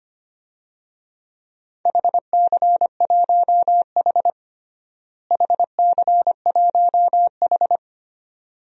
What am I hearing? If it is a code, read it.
Morse code, heard as a keyed tone, about 25 wpm, 700 Hz. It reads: HC1H HC1H
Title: DXCC-100 Call Signs - 4 Characters - Repeated 2x - 25wpm